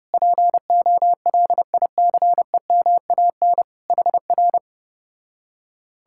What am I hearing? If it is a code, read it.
Morse code, heard as a keyed tone, about 33 wpm, 700 Hz. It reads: POLICEMAN HR